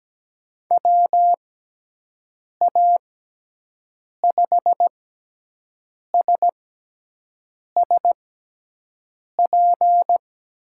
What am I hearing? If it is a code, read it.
Morse code, heard as a keyed tone, about 17 wpm, spaced as for 6 wpm, 700 Hz. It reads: WA5SSP
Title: USA Call Signs - 6 Characters - 6x Character Spacing - 17wpm